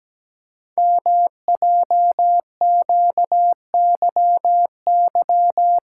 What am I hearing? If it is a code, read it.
Morse code, heard as a keyed tone, about 17 wpm, 700 Hz. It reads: MJQYY
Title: Code Groups - 5 Letters - 17wpm